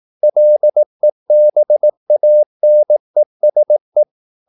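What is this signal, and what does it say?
Morse code, heard as a keyed tone, about 18 wpm, 600 Hz. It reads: LEBANESE
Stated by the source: Morse practice at 18 wpm, English text, from CW med Gnister.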